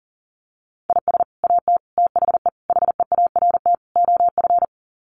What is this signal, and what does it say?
Morse code, heard as a keyed tone, about 40 wpm, 700 Hz. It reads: IS AT THE HEART OF
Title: Sets of 5 Words Encore2 40wpm